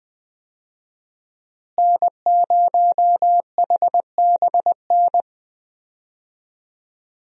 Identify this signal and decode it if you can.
Morse code, heard as a keyed tone, about 21 wpm, 700 Hz. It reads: N0HBN
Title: USA Call Signs - 5 Characters - Repeated 3x - 20wpm